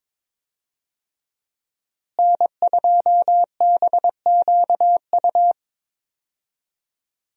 Morse code, heard as a keyed tone, about 22 wpm, 700 Hz. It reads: N2BQU
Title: USA Call Signs - 5 Characters - Repeated 3x - 22wpm